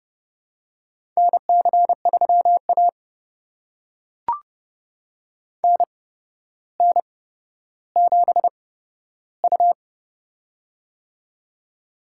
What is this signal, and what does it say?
Morse code, heard as a keyed tone, about 30 wpm, spaced as for 13 wpm, 700 Hz. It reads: NC3A NN7U
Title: USA Call Signs - 4 Characters - 8x Character Spacing - 30wpm